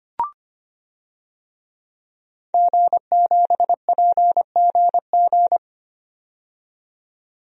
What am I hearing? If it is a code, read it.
Morse code, heard as a keyed tone, about 25 wpm, 700 Hz. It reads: G7PGG
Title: DXCC-100 Call Signs - 5 Characters - 25wpm